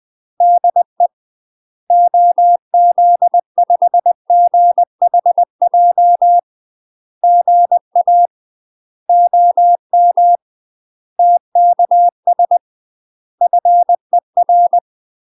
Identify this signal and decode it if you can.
Morse code, heard as a keyed tone, about 20 wpm, 700 Hz. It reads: DE OZ5GHJ GA OM TKS FER
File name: UK MASTER QSOs TRAINING / 20WPM / UK MASTER QSOs 0068